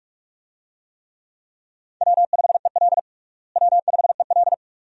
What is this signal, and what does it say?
Morse code, heard as a keyed tone, about 46 wpm, 700 Hz. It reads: W5EL W5EL
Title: USA Call Signs - 4 Characters - Repeated 3x - 45wpm